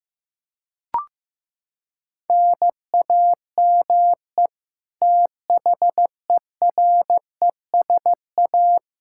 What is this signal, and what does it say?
Morse code, heard as a keyed tone, about 15 wpm, 700 Hz. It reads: NAME THERESA
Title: Practice QSO - Exchange 1 - Beginning - 15wpm